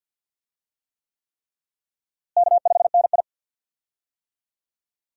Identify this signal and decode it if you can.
Morse code, heard as a keyed tone, about 46 wpm, 700 Hz. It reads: K5NI
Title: USA Call Signs - 4 Characters - Repeated 3x - 50wpm